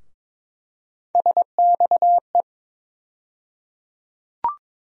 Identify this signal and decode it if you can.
Morse code, heard as a keyed tone, about 22 wpm, 700 Hz. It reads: SXE
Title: USA Call Sign Postfixes - 3 Letters 22wpm